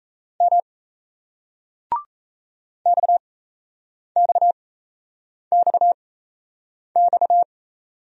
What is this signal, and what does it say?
Morse code, heard as a keyed tone, about 38 wpm, spaced as for 8 wpm, 700 Hz. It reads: M XXXX